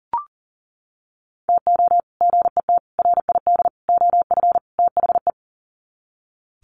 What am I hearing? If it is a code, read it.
Morse code, heard as a keyed tone, about 40 wpm, 700 Hz. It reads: TO GET RID OF THE
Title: Sets of 5 Words 40wpm